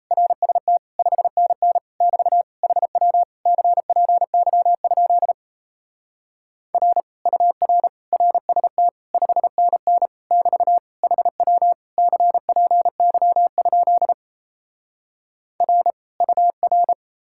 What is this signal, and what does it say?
Morse code, heard as a keyed tone, about 37 wpm, 700 Hz. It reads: RST 5NN <BT> HW CPY? R UR RST 5NN <BT> HW CPY? R UR